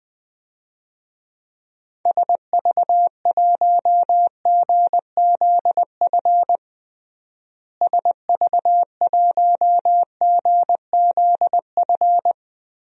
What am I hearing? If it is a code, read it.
Morse code, heard as a keyed tone, about 20 wpm, 700 Hz. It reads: SV1GZF SV1GZF
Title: DXCC-100 Call Signs - 6 Characters - Repeated 3x - 20wpm